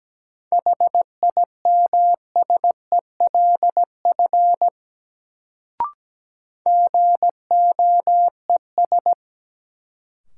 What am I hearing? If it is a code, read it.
Morse code, heard as a keyed tone, about 17 wpm, 700 Hz. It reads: HIMSELF GOES
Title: Sentences from Top 1500 Words - New Word Drill Alt - 17wpm